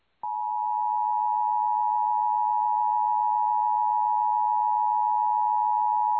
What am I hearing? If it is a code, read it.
Emergency Alert System (SAME) tones (the 853 and 960 Hz attention tone). It is heard from 1.2 to 5.7 s.
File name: EAS Logs / KMNR